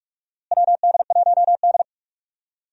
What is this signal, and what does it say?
Morse code, heard as a keyed tone, about 47 wpm, 700 Hz. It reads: WD1D
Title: USA Call Signs - 4 Characters - Repeated 2x - 45wpm